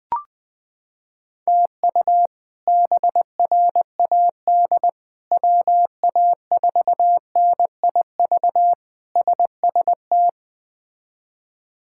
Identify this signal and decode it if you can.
Morse code, heard as a keyed tone, about 20 wpm, 700 Hz. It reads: TU BRAD WA4NIV SST
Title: K1USN SST QSO Practice - 20wpm